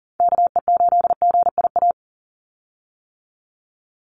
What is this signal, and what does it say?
Morse code, heard as a keyed tone, about 37 wpm, 690 Hz. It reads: KE8GIA
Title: USA Call Signs 40wpm